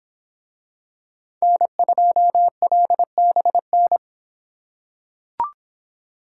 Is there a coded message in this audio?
Yes: Morse code from 1.4 to 4.0 s, heard as a keyed tone, about 26 wpm, 700 Hz. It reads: N2LBN